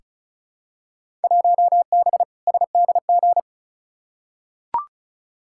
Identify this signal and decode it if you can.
Morse code, heard as a keyed tone, about 36 wpm, 690 Hz. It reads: 1B SDG
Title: Field Day Exchanges Encore - 35wpm